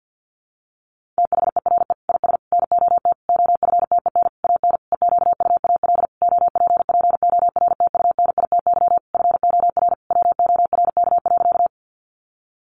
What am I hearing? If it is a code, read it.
Morse code, heard as a keyed tone, about 50 wpm, 700 Hz. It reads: THERE IS NOT OFTEN AN EQUAL OPPORTUNITY FOR WORK.
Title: Sentences from Top 1400 Words 50wpm